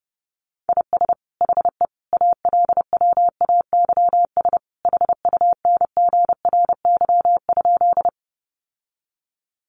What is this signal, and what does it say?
Morse code, heard as a keyed tone, about 30 wpm, 700 Hz. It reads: IS HE ALWAYS HUNGRY?
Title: Sentences from Top 1400 Words 30wpm